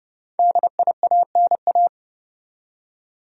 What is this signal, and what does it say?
Morse code, heard as a keyed tone, about 30 wpm, 700 Hz. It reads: DIANA
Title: USA Common Names - Max 5 Letters - 30wpm